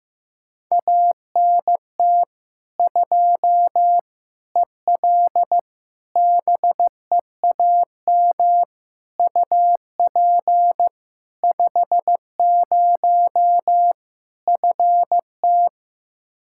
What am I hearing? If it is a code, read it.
Morse code, heard as a keyed tone, about 15 wpm, 700 Hz. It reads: ANT 2 EL BEAM UP 50 FT
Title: Practice QSO - Exchange 2 - Info - 15wpm